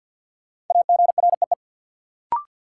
Morse code, heard as a keyed tone, about 53 wpm, 700 Hz. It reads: AGREE
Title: Sets of 5 Words Encore - New Word Drill - 50wpm